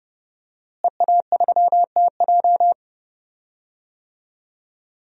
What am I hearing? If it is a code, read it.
Morse code, heard as a keyed tone, about 31 wpm, 700 Hz. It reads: EA3TJ
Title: DXCC-100 Call Signs - 5 Characters - Repeated 3x - 30wpm